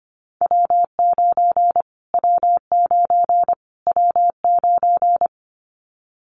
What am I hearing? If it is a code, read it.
Morse code, heard as a keyed tone, about 25 wpm, 700 Hz. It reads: W9 W9 W9
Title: Callsign Prefixes for Top 25 DXCC - Repeated 3x - 25wpm